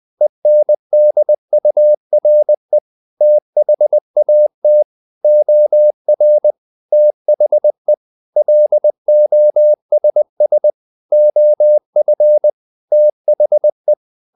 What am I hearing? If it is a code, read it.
Morse code, heard as a keyed tone, about 20 wpm, 600 Hz. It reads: ENDURE THAT OR THE LOSS OF THE
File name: the old man and the sea / the old man and the sea part 1 20wpm